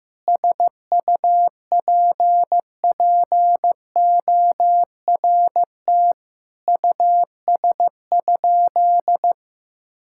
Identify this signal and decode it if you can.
Morse code, heard as a keyed tone, about 15 wpm, 700 Hz. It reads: SUPPORT US?
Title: Sentences from Top 1000 Words - Review Part 5 - 15wpm